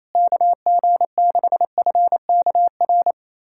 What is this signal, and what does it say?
Morse code, heard as a keyed tone, about 28 wpm, 700 Hz. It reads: KG6FKR
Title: USA Callsign DE Callsign - 6 Characters - 28wpm